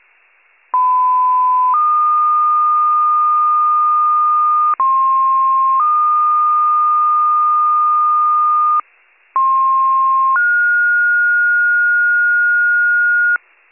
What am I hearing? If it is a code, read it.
Two-tone page, heard as a one-second tone, then a three-second tone, about 1010 then 1220 Hz, about 1060 then 1220 Hz, about 1060 then 1430 Hz, with noise. It is heard from 0.7 to 13.4 s.